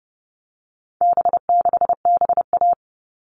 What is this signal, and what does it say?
Morse code, heard as a keyed tone, about 30 wpm, 700 Hz. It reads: B6BA
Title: All Cut Numbers - 4-Digit 30wpm